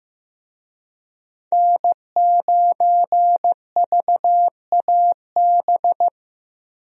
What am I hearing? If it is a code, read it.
Morse code, heard as a keyed tone, about 15 wpm, 700 Hz. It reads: N9VAB
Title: USA Call Signs - 5 Characters - 15wpm